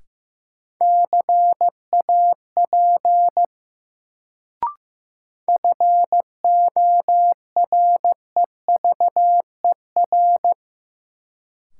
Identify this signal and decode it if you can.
Morse code, heard as a keyed tone, about 15 wpm, 700 Hz. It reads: CAP FOREVER
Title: Sentences from Top 1600 Words - New Word Drill Alt - 15wpm